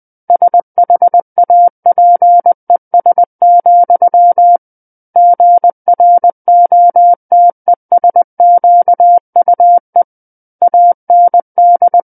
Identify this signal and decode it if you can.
Morse code, heard as a keyed tone, about 20 wpm, 700 Hz. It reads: SHAPES, GROTESQUE AND
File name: 20 WPM